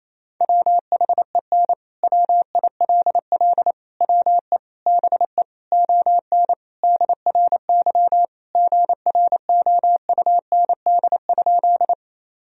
Morse code, heard as a keyed tone, about 28 wpm, 700 Hz. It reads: WHEN WILL WE BE ON DRY GROUND?
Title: Sentences from Top 1000 Words - Review Part 3 - 28wpm